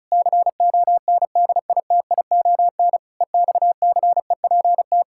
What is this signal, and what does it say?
Morse code, heard as a keyed tone, about 36 wpm, 700 Hz. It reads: CONDITION EXCEPT
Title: Pangram Challenge v2 - 35wpm